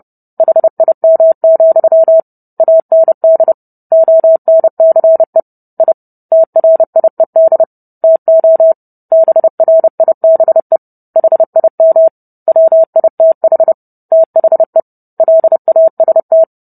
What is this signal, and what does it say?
Morse code, heard as a keyed tone, about 30 wpm, 650 Hz. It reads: HIM, AND ONCE I TRIED TO BRIBE HIM WITH THE LAST